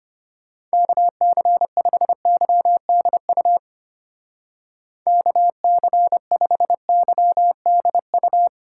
Morse code, heard as a keyed tone, about 32 wpm, 700 Hz. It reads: KC5YDU KC5YDU